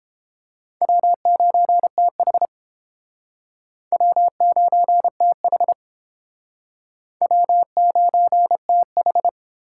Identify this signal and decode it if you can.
Morse code, heard as a keyed tone, about 28 wpm, 700 Hz. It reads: W9TH W9TH W9TH